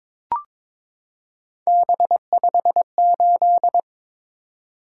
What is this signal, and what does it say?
Morse code, heard as a keyed tone, about 22 wpm, 700 Hz. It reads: B58
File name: practice / 22wpm / Three Letter-Number Combinations - 22wpm